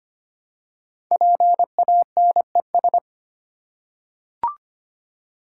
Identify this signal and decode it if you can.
Morse code, heard as a keyed tone, about 26 wpm, 700 Hz. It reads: PANES
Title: Code Groups - 5 Letters - 25wpm